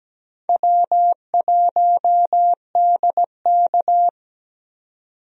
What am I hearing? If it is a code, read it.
Morse code, heard as a keyed tone, about 17 wpm, 700 Hz. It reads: W1DK
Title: USA Call Signs - 4 Characters - 17wpm